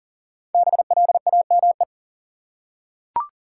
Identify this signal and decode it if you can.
Morse code, heard as a keyed tone, about 40 wpm, 700 Hz. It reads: BLAME
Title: Five Letter Words - 40wpm